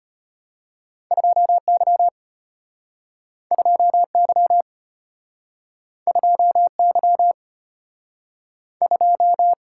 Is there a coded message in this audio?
Yes: Morse code from 1.1 to 9.5 s, heard as a keyed tone, about 33 wpm, 700 Hz. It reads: 2Y 2Y 2Y 2